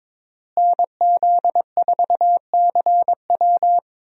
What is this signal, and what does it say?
Morse code, heard as a keyed tone, about 22 wpm, 700 Hz. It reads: NZ4CW
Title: USA Call Signs - 5 Characters - Repeated 2x - 22wpm